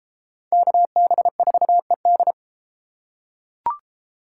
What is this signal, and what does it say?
Morse code, heard as a keyed tone, about 31 wpm, 700 Hz. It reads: KB4ED